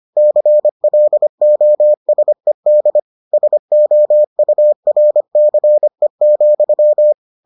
Morse code, heard as a keyed tone, about 25 wpm, 600 Hz. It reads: CLOSED SOURCE,